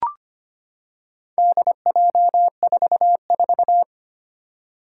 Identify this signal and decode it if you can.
Morse code, heard as a keyed tone, about 25 wpm, 700 Hz. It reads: DJ44
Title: Four Challenging Letter-Number Combinations 25wpm